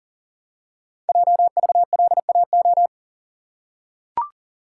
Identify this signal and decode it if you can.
Morse code, heard as a keyed tone, about 40 wpm, 700 Hz. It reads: JVLAO
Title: Code Groups - 5 Letter:Numbers - 40wpm